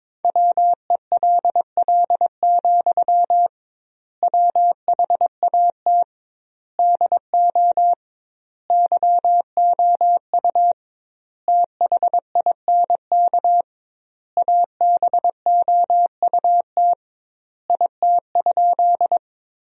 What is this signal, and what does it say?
Morse code, heard as a keyed tone, about 22 wpm, 700 Hz. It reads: WELL, WHAT DO YOU THINK ABOUT IT?
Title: 2x Word-Spacing - Sentences from Top 200 Words - 22wpm